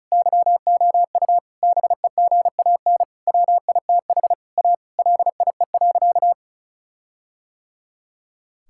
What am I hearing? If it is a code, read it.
Morse code, heard as a keyed tone, about 34 wpm, 700 Hz. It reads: YOU BEGAN WITH A LIE.
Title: Sentences from Top 1600 Words 35wpm